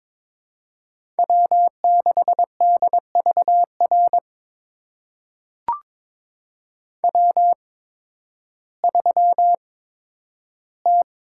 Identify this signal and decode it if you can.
Morse code, heard as a keyed tone, about 22 wpm, spaced as for 12 wpm, 700 Hz. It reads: W6DVR W3T
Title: USA Call Signs - 5 Characters - 8x Character Spacing - 22wpm